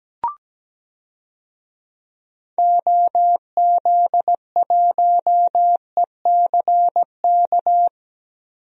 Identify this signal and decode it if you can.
Morse code, heard as a keyed tone, about 17 wpm, 700 Hz. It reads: OZ1ECK
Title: DXCC-100 Call Signs - 6 Characters - Repeated 3x - 17wpm